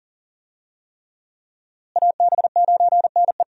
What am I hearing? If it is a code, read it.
Morse code, heard as a keyed tone, about 40 wpm, 700 Hz. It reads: AB9NE
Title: USA Call Signs - 5 Characters - Repeated 3x - 40wpm